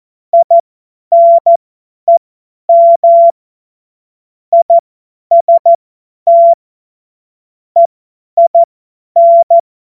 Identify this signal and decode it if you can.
Morse code, heard as a keyed tone, about 14 wpm, spaced as for 10 wpm, 680 Hz. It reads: INEM IST EIN
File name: Morsekurs / Witze / Witz 01. Juni 2021, 50 F70